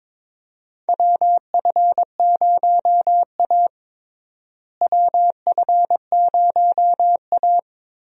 Morse code, heard as a keyed tone, about 21 wpm, 700 Hz. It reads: WF0A WF0A
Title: USA Call Signs - 4 Characters - Repeated 2x - 22wpm